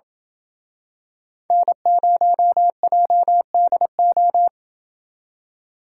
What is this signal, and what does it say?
Morse code, heard as a keyed tone, about 27 wpm, 700 Hz. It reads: N0JDO